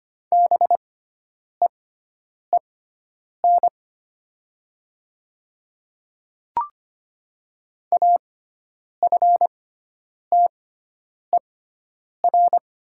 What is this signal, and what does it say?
Morse code, heard as a keyed tone, about 25 wpm, spaced as for 9 wpm, 700 Hz. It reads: BEEN AFTER
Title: 6x Character Spacing - Top 500 Words - 25wpm